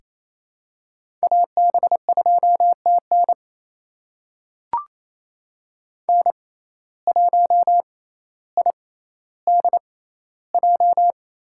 Morse code, heard as a keyed tone, about 27 wpm, spaced as for 14 wpm, 700 Hz. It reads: AB2TN N1IDJ
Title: USA Call Signs - 5 Characters - 6x Character Spacing - 28wpm